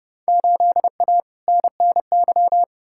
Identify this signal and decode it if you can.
Morse code, heard as a keyed tone, about 30 wpm, 700 Hz. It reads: 8A NNY
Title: Field Day Exchanges Encore - 30wpm